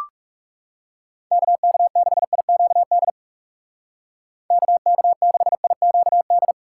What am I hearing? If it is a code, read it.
Morse code, heard as a keyed tone, about 35 wpm, 700 Hz. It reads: KK6IQD KK6IQD
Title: Speed Racing - USA 2x3 Call Signs 30wpm